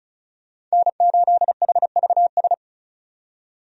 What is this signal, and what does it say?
Morse code, heard as a keyed tone, about 36 wpm, 700 Hz. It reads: N8HVS
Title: USA Call Signs - 5 Characters - 35wpm